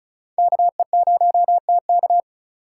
Morse code, heard as a keyed tone, about 36 wpm, 700 Hz. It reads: KE0TK
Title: USA Call Signs - 5 Characters - 4x Character Spacing - 35wpm